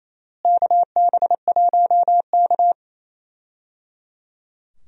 Morse code, heard as a keyed tone, about 28 wpm, 700 Hz. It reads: KB1K
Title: USA Call Signs - 4 Characters - Repeated 3x - 28wpm